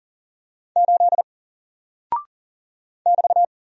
Morse code, heard as a keyed tone, about 40 wpm, 700 Hz. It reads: 8 <BT>
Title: Single Character (Letter:Number:Punctuation:Prosign) Alt 40wpm